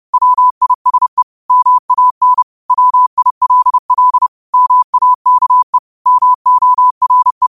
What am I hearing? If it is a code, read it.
Morse code, heard as a keyed tone, about 30 wpm, 1000 Hz. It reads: WISE MAN WILL MAKE MORE
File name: qotdfiles / 30wpm